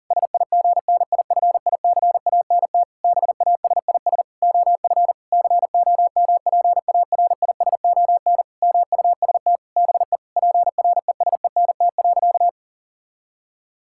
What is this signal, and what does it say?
Morse code, heard as a keyed tone, about 40 wpm, 700 Hz. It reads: SIGNIFICANT BASIS OF COMPARISON MUST BE PRESENT.
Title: Sentences from Top 1600 Words 40wpm